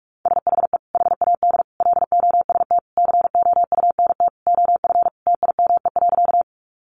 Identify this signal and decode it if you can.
Morse code, heard as a keyed tone, about 44 wpm, 700 Hz. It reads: SHE HAD LOST COUNT OF TIME.
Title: Sentences from Top 1500 Words 45wpm